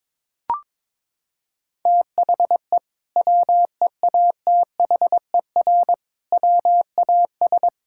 Morse code, heard as a keyed tone, about 22 wpm, 700 Hz. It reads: THE WEATHER WAS
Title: Sentences from Top 1000 Words - Review Part 5 - 22wpm